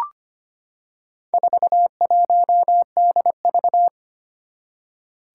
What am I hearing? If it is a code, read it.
Morse code, heard as a keyed tone, about 25 wpm, 700 Hz. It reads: 41DV
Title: Four Challenging Letter-Number Combinations 25wpm